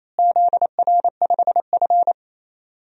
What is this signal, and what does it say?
Morse code, heard as a keyed tone, about 28 wpm, 700 Hz. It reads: ZR5F
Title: DXCC-100 Call Signs - 4 Characters - Repeated 3x - 28wpm